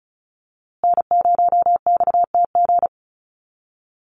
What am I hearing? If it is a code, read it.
Morse code, heard as a keyed tone, about 34 wpm, 700 Hz. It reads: N0XTG
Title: USA Call Signs Repeated 3x 35wpm